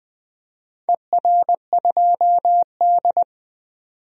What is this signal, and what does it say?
Morse code, heard as a keyed tone, about 20 wpm, 700 Hz. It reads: ER2D
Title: DXCC-100 Call Signs - 4 Characters - 20wpm